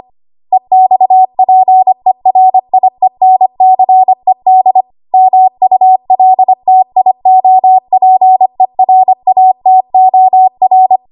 Morse code, heard as a keyed tone, about 25 wpm, 750 Hz. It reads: EXPERIENCED MULTIOPERATOR
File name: Archive / 25 WPM / 25WPM